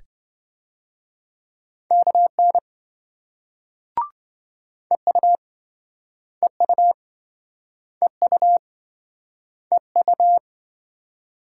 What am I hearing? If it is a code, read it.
Morse code, heard as a keyed tone, about 28 wpm, 700 Hz. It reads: KN EU EU EU EU